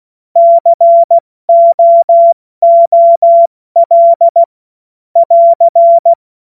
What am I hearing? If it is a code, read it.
Morse code, heard as a keyed tone, about 16 wpm, 680 Hz. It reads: COOL <AR>
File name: Morsekurs / Witze / Witz 08. November 2022 70 F80